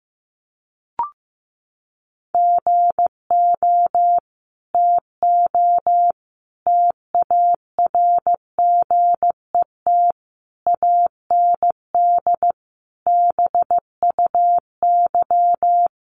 Morse code, heard as a keyed tone, about 15 wpm, 700 Hz. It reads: GO TO TARGET AND BUY
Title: Sentences from Top 1200 Words 15wpm